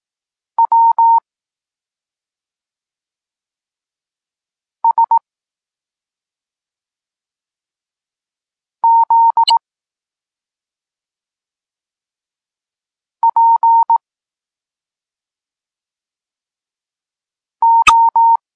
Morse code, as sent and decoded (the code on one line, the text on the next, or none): .-- / ... / --.. / .--. / ---
W S Z P O